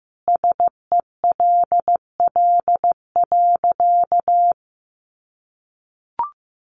Morse code, heard as a keyed tone, about 14 wpm, 700 Hz. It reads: SELL.
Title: Sentences from Top 1100 Words 15wpm